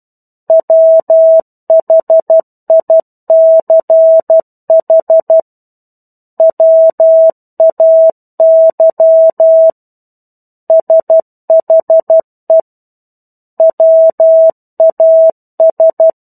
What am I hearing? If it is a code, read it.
Morse code, heard as a keyed tone, about 12 wpm, 650 Hz. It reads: WHICH WAY SHE WAS
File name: 01-Chapter-01 12 WPM